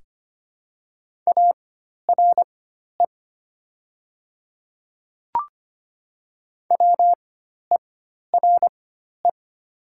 Morse code, heard as a keyed tone, about 25 wpm, spaced as for 12 wpm, 700 Hz. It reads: ARE WERE
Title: 4x Character Spacing - Top 100 Words - 25wpm